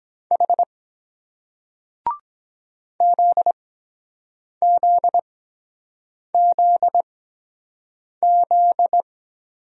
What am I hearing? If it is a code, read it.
Morse code, heard as a keyed tone, about 23 wpm, spaced as for 7 wpm, 700 Hz. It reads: H ZZZZ